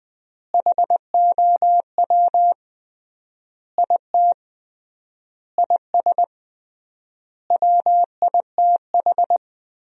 Morse code, heard as a keyed tone, about 20 wpm, 700 Hz. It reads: HOW IT IS WITH
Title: 3x Word-Spacing - Sentences from Top 100 Words - 20wpm